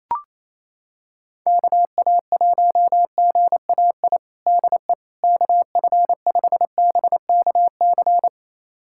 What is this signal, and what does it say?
Morse code, heard as a keyed tone, about 28 wpm, 700 Hz. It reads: KA1GAI DE KF5BKC
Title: USA Callsign DE Callsign - 6 Characters - 28wpm